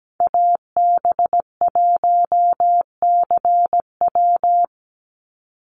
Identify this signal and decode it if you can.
Morse code, heard as a keyed tone, about 17 wpm, 700 Hz. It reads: AB1CW
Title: Call Signs Repeated 2x - 17wpm